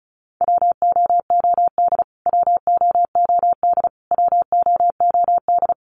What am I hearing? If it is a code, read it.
Morse code, heard as a keyed tone, about 34 wpm, 700 Hz. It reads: WOOD WOOD WOOD
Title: Top 500 Words Repeated 35wpm